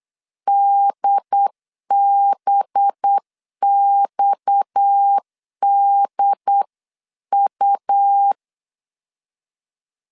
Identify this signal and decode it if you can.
Morse code, heard as a keyed tone, about 8 wpm, 800 Hz. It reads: DBXDU
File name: CW / DET / 1 9 1500C S C 8wpm 5653